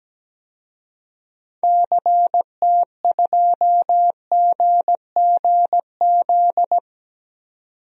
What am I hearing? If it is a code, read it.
Morse code, heard as a keyed tone, about 17 wpm, 700 Hz. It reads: CT2GGZ
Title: DXCC-100 Call Signs - 6 Characters - 17wpm